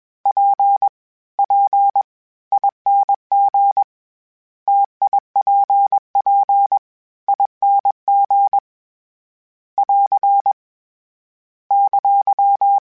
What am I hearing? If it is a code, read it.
Morse code, heard as a keyed tone, about 21 wpm, 800 Hz. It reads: P P ING TIPP ING <AR> !
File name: Difficult Combos / Difficult Combos-mp3s / T AOEI @20